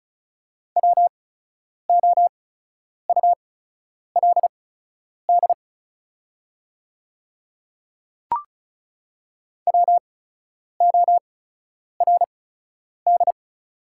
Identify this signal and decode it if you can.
Morse code, heard as a keyed tone, about 36 wpm, spaced as for 10 wpm, 700 Hz. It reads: WOULD WORD
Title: 8x Character Spacing - Top 100 Words - 35wpm